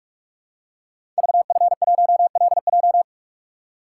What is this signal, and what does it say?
Morse code, heard as a keyed tone, about 44 wpm, 700 Hz. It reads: VF1LJ
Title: Code Groups - 5 Letter:Numbers - 45wpm